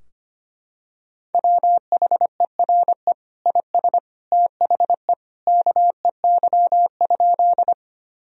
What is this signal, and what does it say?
Morse code, heard as a keyed tone, about 25 wpm, 700 Hz. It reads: WHERE IS THE KEY?
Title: Sentences from Top 1000 Words - Review Part 4 - 25wpm